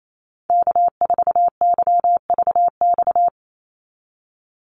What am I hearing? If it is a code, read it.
Morse code, heard as a keyed tone, about 28 wpm, 700 Hz. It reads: K4YVX
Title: USA Call Signs 28wpm